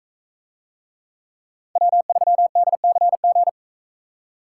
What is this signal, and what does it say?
Morse code, heard as a keyed tone, about 42 wpm, 700 Hz. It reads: W3DCG